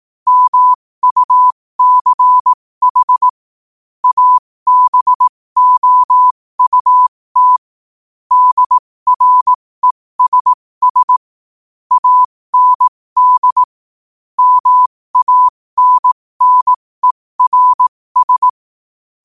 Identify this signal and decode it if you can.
Morse code, heard as a keyed tone, about 18 wpm, spaced as for 15 wpm, 1000 Hz. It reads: MUCH ABOUT DRESS AND MANNERS
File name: qotdfiles / 15wpm